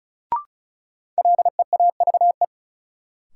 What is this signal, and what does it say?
Morse code, heard as a keyed tone, about 34 wpm, 700 Hz. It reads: LEAVE
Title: Sentences from Top 400 Words - New Word Drill - 35wpm